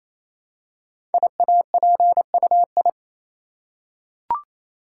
Morse code, heard as a keyed tone, about 27 wpm, 700 Hz. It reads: IAPUI
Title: Code Groups - 5 Letters - 28wpm